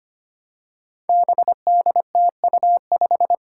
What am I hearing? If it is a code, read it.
Morse code, heard as a keyed tone, about 25 wpm, 700 Hz. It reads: BDTU5
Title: Five Challenging Letter-Number Combinations 25wpm